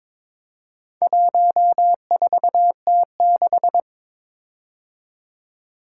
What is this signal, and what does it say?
Morse code, heard as a keyed tone, about 22 wpm, 700 Hz. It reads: 14T6
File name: practice / 22wpm / Four Letter-Number Combinations - 22wpm